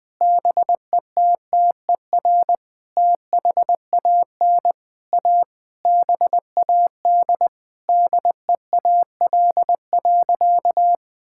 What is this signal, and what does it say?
Morse code, heard as a keyed tone, about 20 wpm, 700 Hz. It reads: BETTER THAN A BAD DEAL.
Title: Sentences with 8-Word Phrases - 20wpm